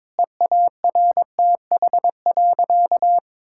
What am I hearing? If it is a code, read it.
Morse code, heard as a keyed tone, about 22 wpm, 700 Hz. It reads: EARTH.
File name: Difficult Sentences with Hints - 22wpm